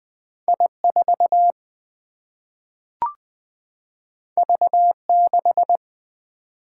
Morse code, heard as a keyed tone, about 20 wpm, 700 Hz. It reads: I 4 V 6
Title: Single-Letter Single-Number 20wpm